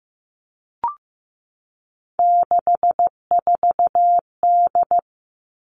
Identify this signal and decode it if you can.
Morse code, heard as a keyed tone, about 15 wpm, 700 Hz. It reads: 64D
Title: Three Challenging Letter-Number Combinations 15wpm